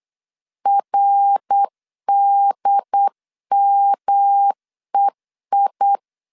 Morse code, heard as a keyed tone, about 8 wpm, 800 Hz. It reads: RDMEI
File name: CW / DET / 1 3 1500C S B 8wpm 4141